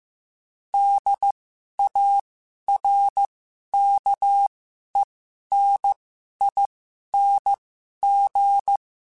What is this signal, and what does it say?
Morse code, heard as a keyed tone, about 15 wpm, spaced as for 11 wpm, 800 Hz. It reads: DARKENING
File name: Special INGs-mp3s / Special INGs-mp3s / Sngl Ing @10